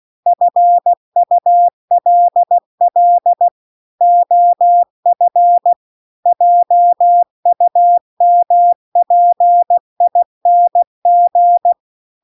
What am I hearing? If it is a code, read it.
Morse code, heard as a keyed tone, about 16 wpm, 700 Hz. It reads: FULL OF JUMPING